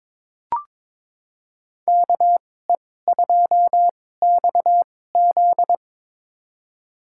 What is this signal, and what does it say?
Morse code, heard as a keyed tone, about 22 wpm, spaced as for 16 wpm, 700 Hz. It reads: KE2XZ
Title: USA Call Signs - 5 Characters - 2x Character Spacing - 22wpm